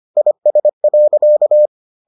Morse code, heard as a keyed tone, about 25 wpm, 600 Hz. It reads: IS.